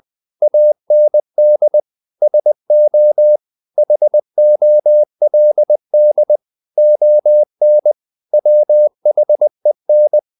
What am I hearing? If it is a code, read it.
Morse code, heard as a keyed tone, about 20 wpm, 600 Hz. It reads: AND SO HOLD ON WHEN
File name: CW / Practice / If20